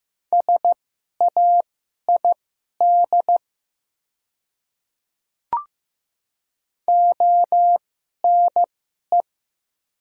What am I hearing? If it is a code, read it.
Morse code, heard as a keyed tone, about 15 wpm, spaced as for 11 wpm, 700 Hz. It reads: SAID ONE
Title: 2x Character Spacing - Top 200 Words - 15wpm